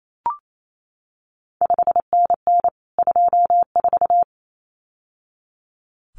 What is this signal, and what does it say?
Morse code, heard as a keyed tone, about 28 wpm, 700 Hz. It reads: 5NN 24
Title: CQ WW DX Contest Exchange Encore - 28wpm